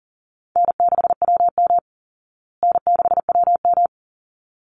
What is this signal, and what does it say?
Morse code, heard as a keyed tone, about 40 wpm, 700 Hz. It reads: N6WM N6WM
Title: Call Signs Repeated 3x - 40wpm